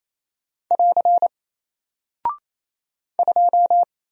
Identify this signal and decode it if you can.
Morse code, heard as a keyed tone, about 28 wpm, 700 Hz. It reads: <AR> 2
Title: Single Character (Letter:Number:Punctuation:Prosign) 28wpm